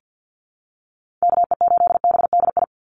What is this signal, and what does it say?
Morse code, heard as a keyed tone, about 50 wpm, 700 Hz. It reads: KE8BDI
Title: USA Call Signs 50wpm